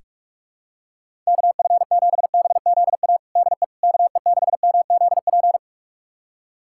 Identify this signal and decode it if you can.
Morse code, heard as a keyed tone, about 46 wpm, 700 Hz. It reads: KF7BZA DE KE6MZP
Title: USA Callsign DE Callsign - 6 Characters - 45wpm